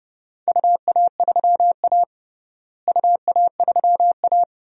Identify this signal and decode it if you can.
Morse code, heard as a keyed tone, about 30 wpm, 700 Hz. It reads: UA3A UA3A
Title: DXCC-100 Call Signs - 4 Characters - Repeated 2x - 30wpm